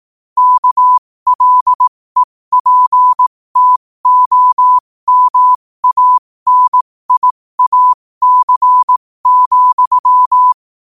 Morse code, heard as a keyed tone, about 18 wpm, spaced as for 15 wpm, 1000 Hz. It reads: KLEPTOMANIAC,